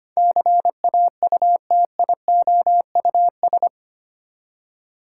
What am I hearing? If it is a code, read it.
Morse code, heard as a keyed tone, about 25 wpm, 700 Hz. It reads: CAUTIOUS
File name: Eight Letter Words - 25wpm